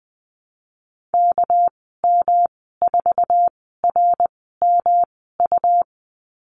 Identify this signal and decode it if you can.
Morse code, heard as a keyed tone, about 20 wpm, spaced as for 15 wpm, 700 Hz. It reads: KM4RMU
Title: 238 Lesson 38g - Letter q - Call Signs 20wpm